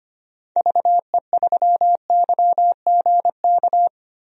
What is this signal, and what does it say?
Morse code, heard as a keyed tone, about 25 wpm, 700 Hz. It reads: VE3YGK
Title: DXCC-100 Call Signs - 6 Characters - Repeated 3x - 25wpm